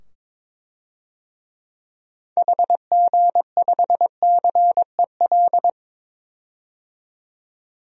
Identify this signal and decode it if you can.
Morse code, heard as a keyed tone, about 22 wpm, 700 Hz. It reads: HG5CEL
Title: DXCC-100 Call Signs - 6 Characters - 22wpm